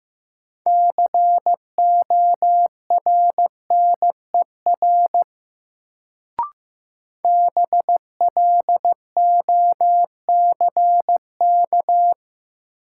Morse code, heard as a keyed tone, about 15 wpm, 700 Hz. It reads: CORNER BLOCK
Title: Sentences from Top 1600 Words - New Word Drill - 15wpm